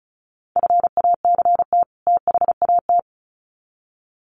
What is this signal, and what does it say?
Morse code, heard as a keyed tone, about 34 wpm, 700 Hz. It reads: FACT THAT
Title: Sets of 2 Words 35wpm